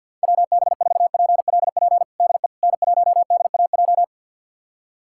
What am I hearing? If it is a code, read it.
Morse code, heard as a keyed tone, about 51 wpm, 700 Hz. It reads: WB4PLP DE N1DAJ
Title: USA Callsign DE Callsign 50wpm